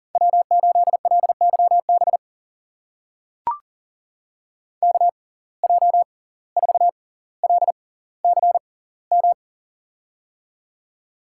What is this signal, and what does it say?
Morse code, heard as a keyed tone, about 40 wpm, spaced as for 19 wpm, 700 Hz. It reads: W8LYB KJ4LCM
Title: USA Call Signs - 6x Character Spacing - 40wpm